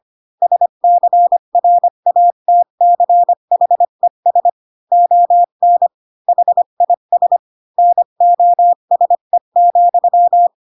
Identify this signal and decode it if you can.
Morse code, heard as a keyed tone, about 25 wpm, 700 Hz. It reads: SCRATCHES ON HIS NOSE,